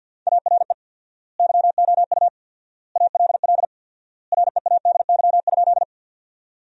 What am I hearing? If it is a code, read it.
Morse code, heard as a keyed tone, about 50 wpm, 700 Hz. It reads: ARE YOU ALL READY?
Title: 4x Word-Spacing - Sentences from Top 400 Words - 50wpm